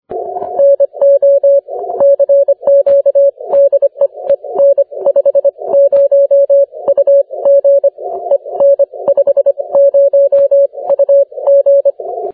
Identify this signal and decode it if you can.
Morse code, heard as a keyed tone, about 25 wpm, spaced as for 21 wpm, 560 Hz. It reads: NO CQ DE EN50UG EN50UG
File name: wp / en50ug 30m cw 12 apr 2011